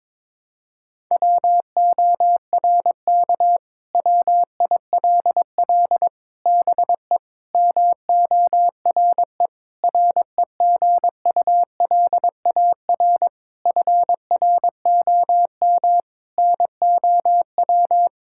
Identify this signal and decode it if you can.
Morse code, heard as a keyed tone, about 22 wpm, 700 Hz. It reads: WORK WILL BE MORE REGULAR FROM NOW
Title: Sentences from Top 1000 Words - Review Part 6 - 22wpm